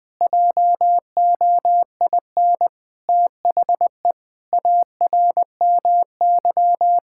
Morse code, heard as a keyed tone, about 21 wpm, 700 Hz. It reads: JOIN THE ARMY
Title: Sentences from Top 1000 Words - Review Part 6 - 20wpm